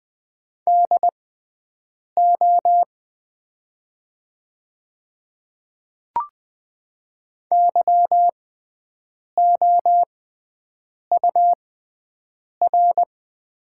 Morse code, heard as a keyed tone, about 20 wpm, spaced as for 7 wpm, 700 Hz. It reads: DO YOUR
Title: 6x Character Spacing - Top 100 Words - 20wpm